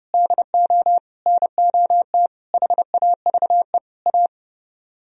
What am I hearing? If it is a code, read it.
Morse code, heard as a keyed tone, about 28 wpm, 700 Hz. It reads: DO NOT HAVE A